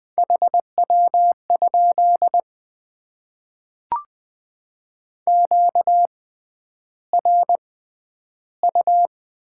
Morse code, heard as a keyed tone, about 20 wpm, spaced as for 10 wpm, 700 Hz. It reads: HW? QRU
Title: Top 90 QSO Elements - 6x Character Spacing - 20wpm